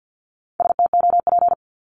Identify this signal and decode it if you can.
Morse code, heard as a keyed tone, about 50 wpm, 700 Hz. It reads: STOP
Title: Multi-letter Phonemes 50wpm